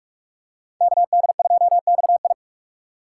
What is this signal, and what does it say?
Morse code, heard as a keyed tone, about 45 wpm, 700 Hz. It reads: KD2XI